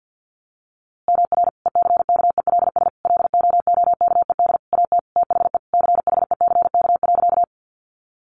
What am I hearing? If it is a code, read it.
Morse code, heard as a keyed tone, about 50 wpm, 700 Hz. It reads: MR ECKELS LOOKED AT THE CHECK.
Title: A Sound Of Thunder 50wpm